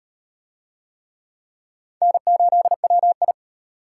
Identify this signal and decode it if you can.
Morse code, heard as a keyed tone, about 38 wpm, 700 Hz. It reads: N8WI